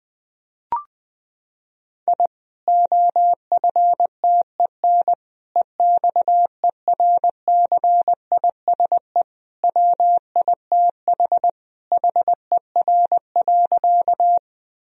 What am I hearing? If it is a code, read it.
Morse code, heard as a keyed tone, about 20 wpm, 700 Hz. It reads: I OFTEN EXERCISE WITH HER.
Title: Sentences from Top 1000 Words - Review Part 4 - 20wpm